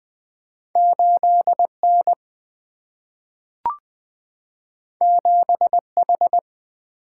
Morse code, heard as a keyed tone, about 20 wpm, 700 Hz. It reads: 8N 7H
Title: Single-Letter Single-Number 20wpm